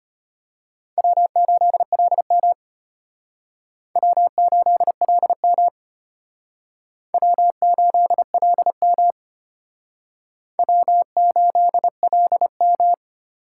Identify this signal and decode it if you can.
Morse code, heard as a keyed tone, about 32 wpm, 700 Hz. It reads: W8LM W8LM W8LM W8LM